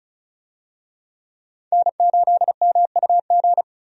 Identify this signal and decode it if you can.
Morse code, heard as a keyed tone, about 36 wpm, 700 Hz. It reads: N8MUG
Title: USA Call Signs - 5 Characters - Repeated 3x - 35wpm